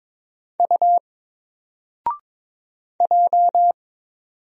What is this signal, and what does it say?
Morse code, heard as a keyed tone, about 22 wpm, 700 Hz. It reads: U J
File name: Single-Letter-Number-22wpm